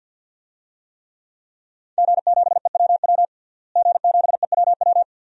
Morse code, heard as a keyed tone, about 52 wpm, 700 Hz. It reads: G7EWW G7EWW
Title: DXCC-100 Call Signs - 5 Characters - Repeated 2x - 50wpm